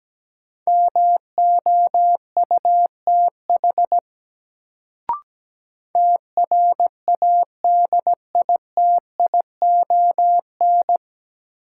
Morse code, heard as a keyed tone, about 17 wpm, 700 Hz. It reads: MOUTH TRADITION
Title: Sentences from Top 1500 Words - New Word Drill - 17wpm